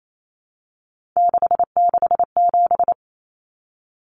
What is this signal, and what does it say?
Morse code, heard as a keyed tone, about 28 wpm, 700 Hz. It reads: 667
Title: Common Cut Numbers - 3-Digit 28wpm